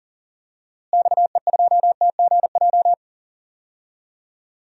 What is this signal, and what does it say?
Morse code, heard as a keyed tone, about 38 wpm, 700 Hz. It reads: XE2TGJ
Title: DXCC-100 Call Signs - 6 Characters - Repeated 2x - 40wpm